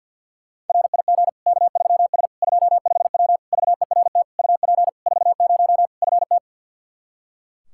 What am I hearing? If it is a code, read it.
Morse code, heard as a keyed tone, about 47 wpm, 700 Hz. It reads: RIG K3S 25W VERT UP 40 FT
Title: K1USN SST QSO Practice 50wpm